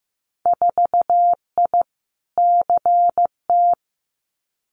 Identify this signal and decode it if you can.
Morse code, heard as a keyed tone, about 15 wpm, 700 Hz. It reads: 4I CT
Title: Winter Field Day Exchanges Encore 15wpm